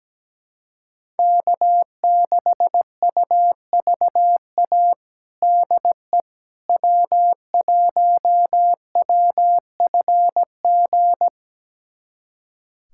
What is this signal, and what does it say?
Morse code, heard as a keyed tone, about 17 wpm, 700 Hz. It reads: K6UVA DE W1WFG
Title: USA Callsign DE Callsign - 5 Characters - 17wpm